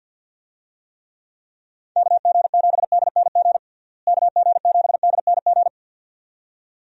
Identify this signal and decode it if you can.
Morse code, heard as a keyed tone, about 50 wpm, 700 Hz. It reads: KG7DNG KG7DNG
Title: USA Call Signs - 6 Characters - Repeated 2x - 50wpm